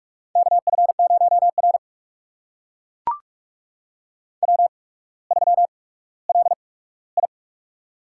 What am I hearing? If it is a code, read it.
Morse code, heard as a keyed tone, about 44 wpm, spaced as for 19 wpm, 700 Hz. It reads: KF0R W3LI
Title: USA Call Signs - 4 Characters - 8x Character Spacing - 45wpm